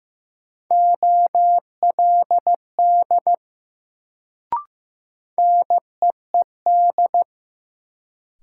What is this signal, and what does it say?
Morse code, heard as a keyed tone, about 15 wpm, 700 Hz. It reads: OLD NEED
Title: Sentences from Top 200 Words - New Word Drill - 15wpm